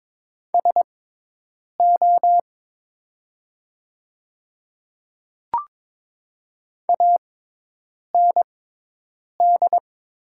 Morse code, heard as a keyed tone, about 22 wpm, spaced as for 8 wpm, 700 Hz. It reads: SO AND
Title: 6x Character Spacing - Top 100 Words - 22wpm